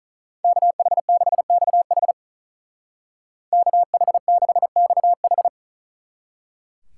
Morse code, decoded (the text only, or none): KH6XH KH6XH